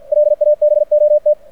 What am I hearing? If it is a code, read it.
Morse code, heard as a keyed tone, about 49 wpm, 600 Hz. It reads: LABOT
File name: archived / CRSnet